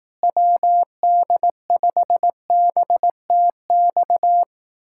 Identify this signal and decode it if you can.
Morse code, heard as a keyed tone, about 18 wpm, 700 Hz. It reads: WD5BTX
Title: Speed-Racing Common QSO Phrases Encore - 15wpm